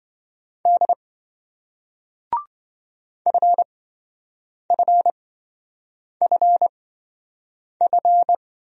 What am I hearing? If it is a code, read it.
Morse code, heard as a keyed tone, about 28 wpm, spaced as for 7 wpm, 700 Hz. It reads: D FFFF